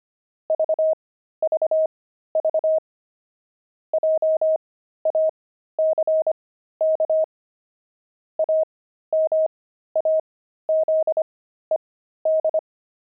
Morse code, heard as a keyed tone, about 25 wpm, spaced as for 13 wpm, 630 Hz. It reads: VVV JACK AMAZED
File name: pangram-fs-13-cs-25-antique